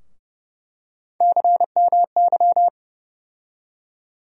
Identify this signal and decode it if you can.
Morse code, heard as a keyed tone, about 30 wpm, 700 Hz. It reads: CMY
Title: USA Call Sign Postfixes - 3 Letters 30wpm